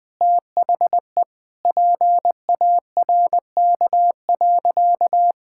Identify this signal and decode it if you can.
Morse code, heard as a keyed tone, about 20 wpm, 700 Hz. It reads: THE PARK.
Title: Easy Sentences with Hints - 20wpm